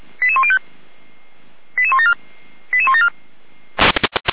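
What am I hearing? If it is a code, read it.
Five-tone selective call (ZVEI), heard as quick runs of five tones, tones of about 70 ms. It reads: 88105 88174 88174
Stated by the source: ZVEI five-tone selective call received on SDR.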